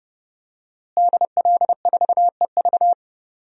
Morse code, heard as a keyed tone, about 30 wpm, 700 Hz. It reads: DL4EV
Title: DXCC-100 Call Signs - 5 Characters - Repeated 3x - 30wpm